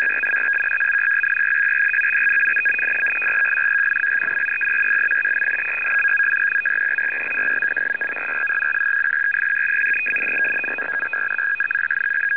Two-tone teleprinter signal, about 850 Hz shift.
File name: USAF-150Bd